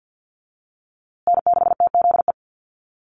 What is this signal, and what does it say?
Morse code, heard as a keyed tone, about 50 wpm, 700 Hz. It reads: N6TZE
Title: USA Call Signs Repeated 2x 50wpm